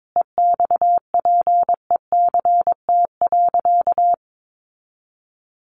Morse code, heard as a keyed tone, about 22 wpm, 700 Hz. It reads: EXPECT.
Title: Sentences from Top 1100 Words 22wpm